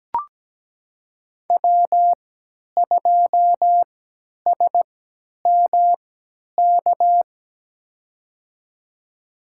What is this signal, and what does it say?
Morse code, heard as a keyed tone, about 17 wpm, spaced as for 10 wpm, 700 Hz. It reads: W2SMK
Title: USA Call Signs - 3x Character Spacing - 17wpm